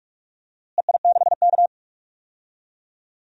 Morse code, heard as a keyed tone, about 47 wpm, 700 Hz. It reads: EI6K
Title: DXCC-100 Call Signs - 4 Characters - Repeated 2x - 45wpm